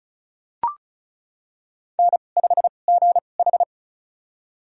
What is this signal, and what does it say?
Morse code, heard as a keyed tone, about 36 wpm, spaced as for 25 wpm, 700 Hz. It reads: N5GH
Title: USA Call Signs - 4 Characters - 2x Character Spacing - 35wpm